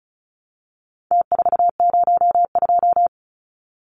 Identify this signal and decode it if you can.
Morse code, heard as a keyed tone, about 34 wpm, 700 Hz. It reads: T402
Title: Numbers - Contest Style - 4-Digit 35wpm